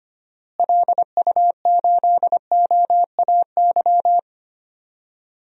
Morse code, heard as a keyed tone, about 25 wpm, 700 Hz. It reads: LU8OAY
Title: DXCC-100 Call Signs - 6 Characters - Repeated 3x - 25wpm